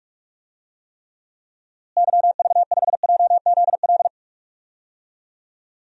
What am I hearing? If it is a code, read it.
Morse code, heard as a keyed tone, about 43 wpm, 700 Hz. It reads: YV5JZL
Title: DXCC-100 Call Signs - 6 Characters - Repeated 2x - 45wpm